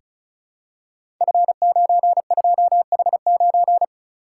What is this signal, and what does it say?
Morse code, heard as a keyed tone, about 36 wpm, 700 Hz. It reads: F92H9
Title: Code Groups - 5 Letter:Numbers - 35wpm